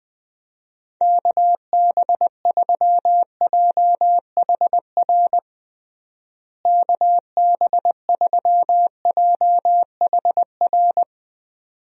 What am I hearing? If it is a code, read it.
Morse code, heard as a keyed tone, about 19 wpm, 700 Hz. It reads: KB3JHR KB3JHR
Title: USA Call Signs - 6 Characters - Repeated 2x - 20wpm